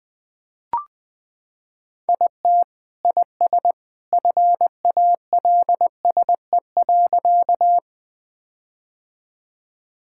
Morse code, heard as a keyed tone, about 20 wpm, 700 Hz. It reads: IT IS FALSE.
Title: Sentences from Top 1600 Words 20wpm